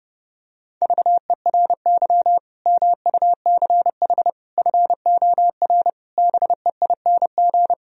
Morse code, heard as a keyed tone, about 29 wpm, 700 Hz. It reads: VERY MUCH FOR BEING